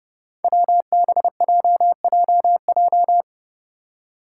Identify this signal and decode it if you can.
Morse code, heard as a keyed tone, about 30 wpm, 700 Hz. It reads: WBJJJ
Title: Five Challenging Letter-Number Combinations 30wpm